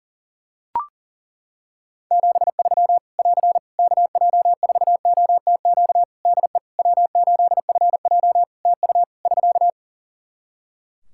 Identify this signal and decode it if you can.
Morse code, heard as a keyed tone, about 41 wpm, 700 Hz. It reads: 73 <AR> KJ4OTQ DE W8FJ TU <SK>
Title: Practice QSO - Exchange 4 - Ending - 40wpm